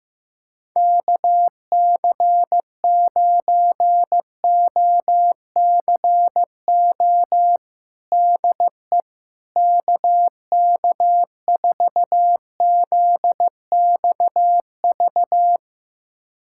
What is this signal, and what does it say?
Morse code, heard as a keyed tone, about 15 wpm, 700 Hz. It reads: KC9OCO DE KK4ZXV
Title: USA Callsign DE Callsign 15wpm